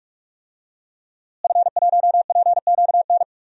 Morse code, heard as a keyed tone, about 46 wpm, 700 Hz. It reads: F1PQN
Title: DXCC-100 Call Signs - 5 Characters - Repeated 3x - 45wpm